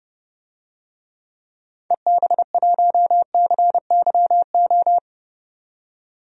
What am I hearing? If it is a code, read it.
Morse code, heard as a keyed tone, about 30 wpm, 700 Hz. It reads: EB1CYO
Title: DXCC-100 Call Signs - 6 Characters - 30wpm